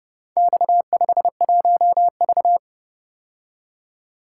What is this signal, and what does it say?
Morse code, heard as a keyed tone, about 32 wpm, 700 Hz. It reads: X51V